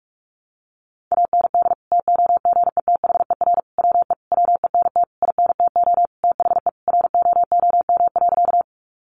Morse code, heard as a keyed tone, about 44 wpm, 700 Hz. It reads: AND TOGETHER WE WENT INTO THE ROOM.
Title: Sentences from Top 600 Words 45wpm